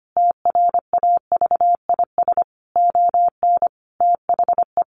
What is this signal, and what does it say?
Morse code, heard as a keyed tone, about 25 wpm, 700 Hz. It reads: TRAVIS ON THE
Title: A Sound Of Thunder 25wpm